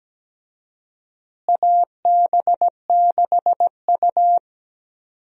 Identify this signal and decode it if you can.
Morse code, heard as a keyed tone, about 17 wpm, 700 Hz. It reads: AB6U
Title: USA Call Signs - 4 Characters - Repeated 3x - 17wpm